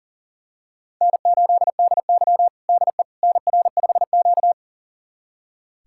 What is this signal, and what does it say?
Morse code, heard as a keyed tone, about 40 wpm, 700 Hz. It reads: N8DY DE NR5Q
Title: USA Callsign DE Callsign - 4 Characters - 40wpm